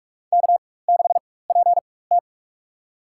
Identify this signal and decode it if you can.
Morse code, heard as a keyed tone, about 46 wpm, spaced as for 21 wpm, 700 Hz. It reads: K6PT
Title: USA Call Signs - 4 Characters - 4x Character Spacing - 45wpm